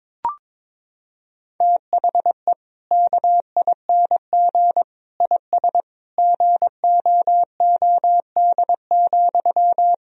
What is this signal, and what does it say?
Morse code, heard as a keyed tone, about 22 wpm, 700 Hz. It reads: THE KING IS GOOD,
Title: Sentences from Top 1000 Words - Review Part 1 - 22wpm